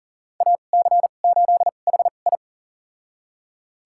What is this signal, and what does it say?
Morse code, heard as a keyed tone, about 40 wpm, spaced as for 29 wpm, 700 Hz. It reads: AC8HI
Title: USA Call Signs - 2x Character Spacing - 40wpm